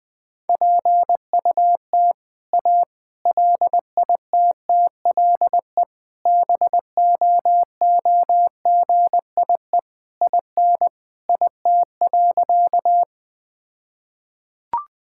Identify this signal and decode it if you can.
Morse code, heard as a keyed tone, about 20 wpm, 700 Hz. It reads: PUT A LITTLE BOOGIE IN IT.
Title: All-Age Appropriate Jokes - 20wpm